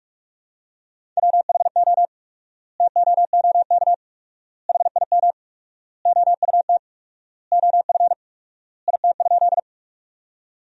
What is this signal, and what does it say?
Morse code, heard as a keyed tone, about 46 wpm, 700 Hz. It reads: WHO TOOK HIM OUT OF IT?
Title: 4x Word-Spacing - Sentences from Top 300 Words - 45wpm